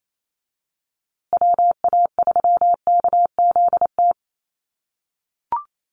Morse code, heard as a keyed tone, about 28 wpm, 700 Hz. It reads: WA3KZT
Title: USA Call Signs Repeated 3x 28wpm